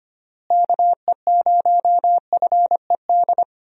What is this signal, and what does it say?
Morse code, heard as a keyed tone, about 25 wpm, 700 Hz. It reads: KE0FED
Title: USA Call Signs - 6 Characters - Repeated 3x - 25wpm